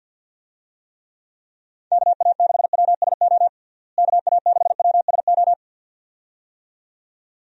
Morse code, heard as a keyed tone, about 51 wpm, 700 Hz. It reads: KA6WSO KA6WSO
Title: USA Call Signs - 6 Characters - Repeated 2x - 50wpm